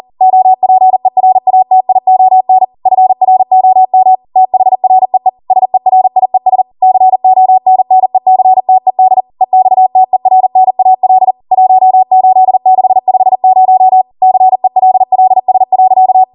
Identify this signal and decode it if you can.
Morse code, heard as a keyed tone, about 40 wpm, 750 Hz. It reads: OPERATION FROM THREE SERIES CONNECTED EXTERNAL 18650 CELLS.
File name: Morse / 40WPM